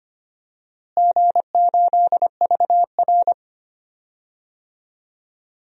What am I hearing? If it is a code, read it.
Morse code, heard as a keyed tone, about 25 wpm, 700 Hz. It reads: G8VR
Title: DXCC-100 Call Signs - 4 Characters - Repeated 3x - 25wpm